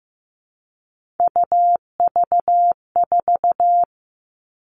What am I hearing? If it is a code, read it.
Morse code, heard as a keyed tone, about 15 wpm, 700 Hz. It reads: UV4
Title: All Cut Numbers - 3-Digit 15wpm